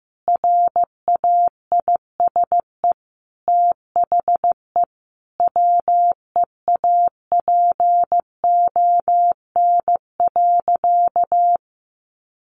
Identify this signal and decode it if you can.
Morse code, heard as a keyed tone, about 15 wpm, 700 Hz. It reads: RAISE THE WEAPON.
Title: A Sound Of Thunder 15wpm